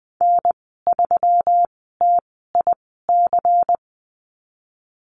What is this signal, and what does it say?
Morse code, heard as a keyed tone, about 20 wpm, spaced as for 15 wpm, 700 Hz. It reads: N3TIC
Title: 167 Lesson 27g - Number 7 - Call Signs 20wpm